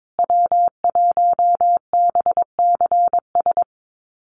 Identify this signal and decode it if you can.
Morse code, heard as a keyed tone, about 22 wpm, 700 Hz. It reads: W1BCS
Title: USA Call Signs Repeated 3x 22wpm